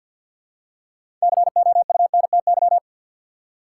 Morse code, heard as a keyed tone, about 52 wpm, 700 Hz. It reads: COUNTY
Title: Six Letter Words - 50wpm